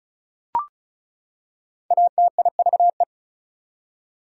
Morse code, heard as a keyed tone, about 34 wpm, 700 Hz. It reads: ATIVE
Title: Prefixes and Suffixes 35wpm